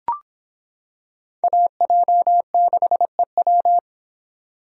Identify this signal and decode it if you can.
Morse code, heard as a keyed tone, about 25 wpm, 700 Hz. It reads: AJ6EW